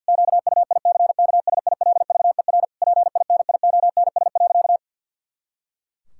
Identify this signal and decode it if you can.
Morse code, heard as a keyed tone, about 50 wpm, 700 Hz. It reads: QUICKSILVER PINIONS.
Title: Pangram Challenge v2 - 50wpm